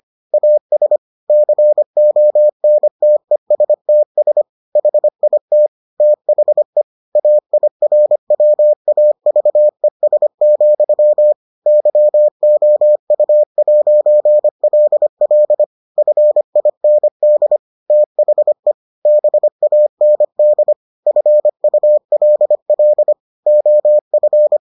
Morse code, heard as a keyed tone, about 25 wpm, 600 Hz. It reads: AS CONTESTS HIT THE AIRWAVES, YOU'LL FIND THE BAND FULL OF